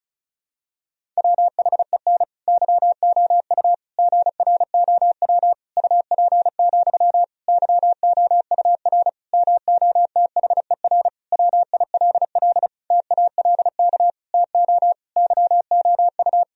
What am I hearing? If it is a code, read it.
Morse code, heard as a keyed tone, about 36 wpm, 700 Hz. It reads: WHEN YOU GROW UP, YOUR MOTHER WILL TALK TO YOU
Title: Sentences from Top 1000 Words - Review Part 1 - 35wpm